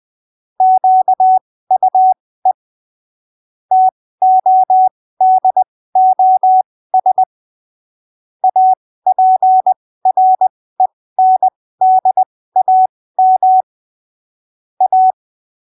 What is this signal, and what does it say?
Morse code, heard as a keyed tone, about 20 wpm, spaced as for 15 wpm, 750 Hz. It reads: QUE TODOS APRENDAM A
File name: arquivos / texto / texto 25 20ppm